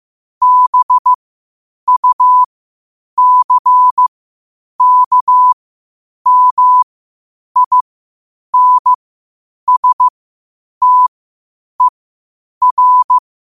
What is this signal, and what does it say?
Morse code, heard as a keyed tone, about 15 wpm, spaced as for 8 wpm, 1000 Hz. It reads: BUCKMINSTER
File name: qotdfiles / 8wpm